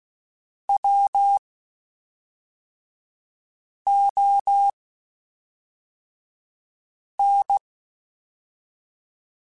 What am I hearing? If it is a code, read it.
Morse code, heard as a keyed tone, about 16 wpm, spaced as for 3 wpm, 800 Hz. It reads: WON